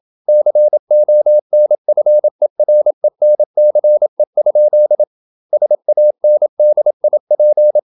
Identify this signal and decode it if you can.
Morse code, heard as a keyed tone, about 27 wpm, 600 Hz. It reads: CONFERENCE? SANDIP
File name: Never 0027wpm